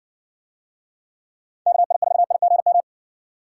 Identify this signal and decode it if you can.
Morse code, heard as a keyed tone, about 60 wpm, 700 Hz. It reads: KI4IGM